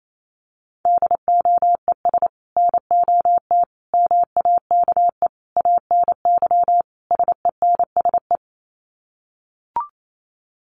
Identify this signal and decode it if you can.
Morse code, heard as a keyed tone, about 28 wpm, 700 Hz. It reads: DOES NOT MAKE ANY SENSE
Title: Sets of 5 Words Encore2 28wpm